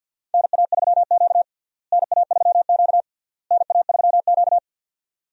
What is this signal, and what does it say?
Morse code, heard as a keyed tone, about 51 wpm, 700 Hz. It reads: NA3Q NA3Q NA3Q
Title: USA Call Signs - 4 Characters - Repeated 3x - 50wpm